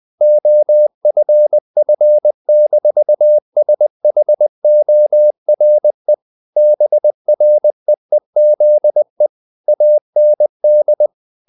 Morse code, heard as a keyed tone, about 20 wpm, 600 Hz. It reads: OFF-SHORE BREEZE AND
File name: the old man and the sea part 2 20wpm